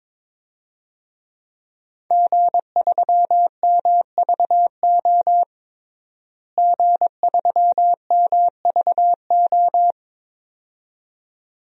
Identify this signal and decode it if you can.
Morse code, heard as a keyed tone, about 22 wpm, 700 Hz. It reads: G3MVO G3MVO
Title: DXCC-100 Call Signs - 5 Characters - Repeated 2x - 22wpm